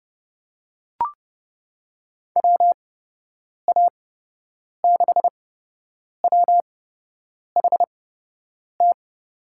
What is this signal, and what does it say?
Morse code, heard as a keyed tone, about 29 wpm, spaced as for 8 wpm, 700 Hz. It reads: WA6WHT
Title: USA Call Signs - 8x Character Spacing - 30wpm